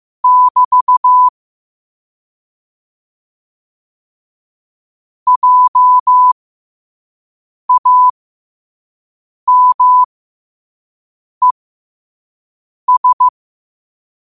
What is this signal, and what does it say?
Morse code, heard as a keyed tone, about 15 wpm, spaced as for 5 wpm, 1000 Hz. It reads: <BT> JAMES
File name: qotdfiles / 5wpm